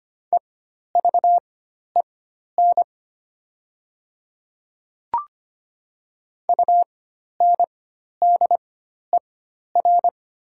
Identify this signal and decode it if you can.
Morse code, heard as a keyed tone, about 25 wpm, spaced as for 12 wpm, 700 Hz. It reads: EVEN UNDER
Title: 4x Character Spacing - Top 200 Words - 25wpm